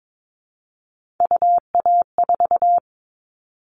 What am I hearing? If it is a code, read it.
Morse code, heard as a keyed tone, about 22 wpm, 700 Hz. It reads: UA4
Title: Callsign Prefixes for Top 25 DXCC - 22wpm